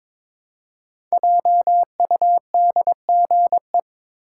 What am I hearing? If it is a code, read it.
Morse code, heard as a keyed tone, about 22 wpm, 700 Hz. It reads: JUDGE
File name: 5 Letter Words - First 3 Letters Given - 22wpm